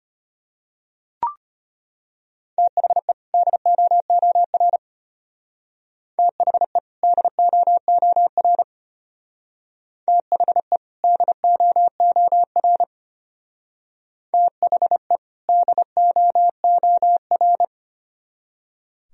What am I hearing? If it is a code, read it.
Morse code, heard as a keyed tone, about 31 wpm, 700 Hz. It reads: THE DOOR THE DOOR THE DOOR THE DOOR
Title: Speed-Racing - Sets of 2 Words 25wpm